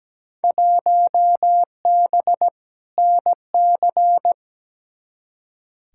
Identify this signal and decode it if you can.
Morse code, heard as a keyed tone, about 17 wpm, 700 Hz. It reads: 1B NC
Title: Field Day Exchanges - 17wpm